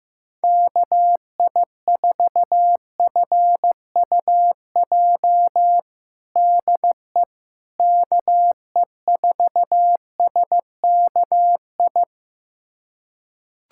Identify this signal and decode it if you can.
Morse code, heard as a keyed tone, about 15 wpm, 700 Hz. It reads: KI4FUJ DE KE4SKI
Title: USA Callsign DE Callsign - 6 Characters - 15wpm